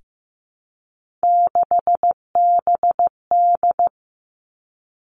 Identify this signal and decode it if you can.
Morse code, heard as a keyed tone, about 15 wpm, 700 Hz. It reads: 6BD
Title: Three Challenging Letter-Number Combinations 15wpm